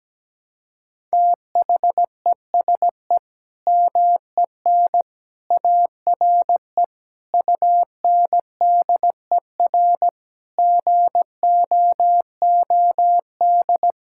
Morse code, heard as a keyed tone, about 17 wpm, 700 Hz. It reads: THESE MEN ARE UNDER GOOD
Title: Sentences from Top 1000 Words - Review Part 6 - 17wpm